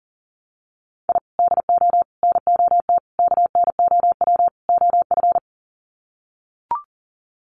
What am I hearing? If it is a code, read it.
Morse code, heard as a keyed tone, about 40 wpm, 700 Hz. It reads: I DO NOT KNOW OF
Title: Sets of 5 Words Encore2 40wpm